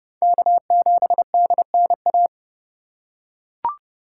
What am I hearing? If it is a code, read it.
Morse code, heard as a keyed tone, about 30 wpm, 700 Hz. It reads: K7DNA
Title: USA Call Signs - 5 Characters - 4x Character Spacing - 30wpm